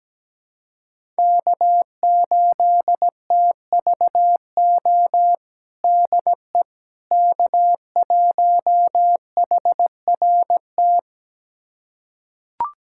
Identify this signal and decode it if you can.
Morse code, heard as a keyed tone, about 17 wpm, 700 Hz. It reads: K8TVO DE K1HRT
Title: USA Callsign DE Callsign - 5 Characters - 17wpm